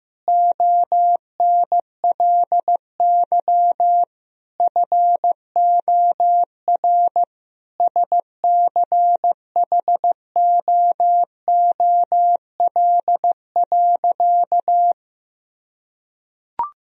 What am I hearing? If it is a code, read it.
Morse code, heard as a keyed tone, about 15 wpm, 700 Hz. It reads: ONLY FOR SCHOOL.
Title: Sentences from Top 1000 Words - Review Part 1 - 15wpm